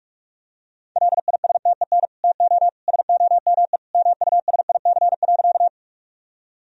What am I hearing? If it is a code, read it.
Morse code, heard as a keyed tone, about 41 wpm, 700 Hz. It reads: LISTEN TO SOME MUSIC.
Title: Sentences from Top 1000 Words - Review Part 1 - 45wpm